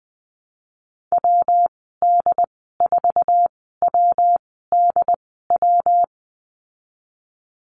Morse code, heard as a keyed tone, about 20 wpm, spaced as for 14 wpm, 700 Hz. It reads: WD4WDW
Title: 122 Lesson 20g - Number 3 - Call Signs 20wpm